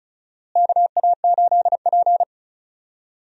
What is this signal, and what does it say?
Morse code, heard as a keyed tone, about 36 wpm, 700 Hz. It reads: KA8P
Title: USA Call Signs - 4 Characters - Repeated 2x - 35wpm